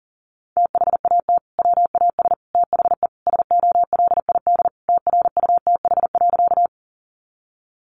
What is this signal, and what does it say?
Morse code, heard as a keyed tone, about 40 wpm, 700 Hz. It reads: THAT WAS THE SOLID TRUTH.
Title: Sentences from Top 1500 Words 40wpm